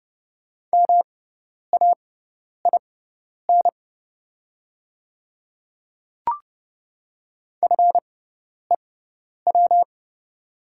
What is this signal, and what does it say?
Morse code, heard as a keyed tone, about 30 wpm, spaced as for 10 wpm, 700 Hz. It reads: MAIN FEW
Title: 6x Character Spacing - Top 500 Words - 30wpm